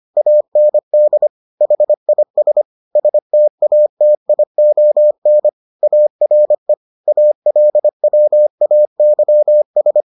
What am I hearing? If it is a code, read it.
Morse code, heard as a keyed tone, about 25 wpm, 600 Hz. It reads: AND HIS STATION ARE ALWAYS